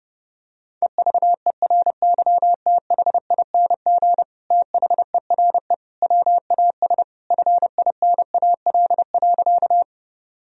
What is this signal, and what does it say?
Morse code, heard as a keyed tone, about 30 wpm, 700 Hz. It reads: EVERYTHING THERE WAS FINAL.
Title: Sentences from Top 1000 Words - Review Part 2 - 30wpm